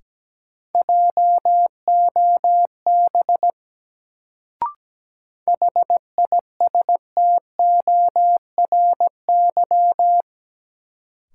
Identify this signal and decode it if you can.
Morse code, heard as a keyed tone, about 17 wpm, 700 Hz. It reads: JOB HISTORY
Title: Sentences from Top 600 Words - New Word Drill - 17wpm